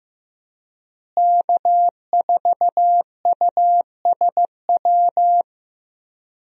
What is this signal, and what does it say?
Morse code, heard as a keyed tone, about 15 wpm, 700 Hz. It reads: K4USW
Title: USA Call Signs - 5 Characters - Repeated 3x - 15wpm